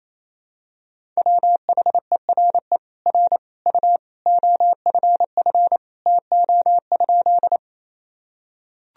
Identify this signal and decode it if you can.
Morse code, heard as a keyed tone, about 28 wpm, 700 Hz. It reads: WHERE R U OFF TO?
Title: Sentences from Top 200 Words and Abbreviations 28wpm